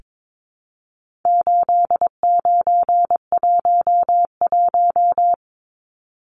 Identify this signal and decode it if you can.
Morse code, heard as a keyed tone, about 22 wpm, 700 Hz. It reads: 8911
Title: Numbers - 4-Digit 22wpm